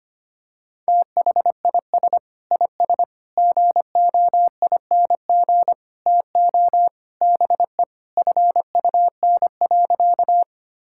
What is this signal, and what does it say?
Morse code, heard as a keyed tone, about 25 wpm, 700 Hz. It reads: THIS IS GOING TO BE FUN.
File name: practice / 25wpm / Sentences with 4-Word Phrases - 25wpm